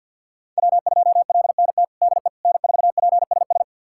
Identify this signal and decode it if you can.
Morse code, heard as a keyed tone, about 49 wpm, 700 Hz. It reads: W2LNT DE N4PSS
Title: USA Callsign DE Callsign - 5 Characters - 50wpm